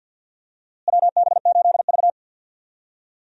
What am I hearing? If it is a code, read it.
Morse code, heard as a keyed tone, about 51 wpm, 700 Hz. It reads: WB8V
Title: USA Call Signs - 4 Characters - Repeated 3x - 50wpm